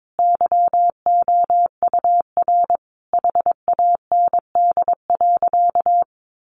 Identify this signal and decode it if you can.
Morse code, heard as a keyed tone, about 22 wpm, 700 Hz. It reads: YOUR HAND.
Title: Sentences from Top 1500 Words 22wpm